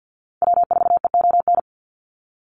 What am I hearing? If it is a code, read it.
Morse code, heard as a keyed tone, about 50 wpm, 700 Hz. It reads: W4EON
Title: NA SOTA Activator Callsigns 50wpm